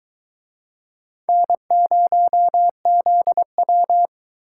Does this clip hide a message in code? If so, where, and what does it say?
Yes: Morse code from 1.3 to 4.1 s, heard as a keyed tone, about 22 wpm, 700 Hz. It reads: N0ZW